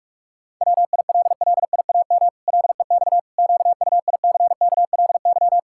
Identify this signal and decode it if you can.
Morse code, heard as a keyed tone, about 48 wpm, 700 Hz. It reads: WILLIAM LEX QUICKLY
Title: Pangram Challenge v2 - 45wpm